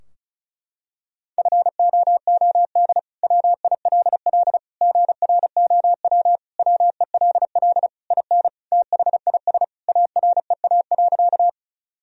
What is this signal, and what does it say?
Morse code, heard as a keyed tone, about 36 wpm, 700 Hz. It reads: FOOD WILL GROW WELL IN THIS AREA.
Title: Sentences from Top 1000 Words - Review Part 2 - 35wpm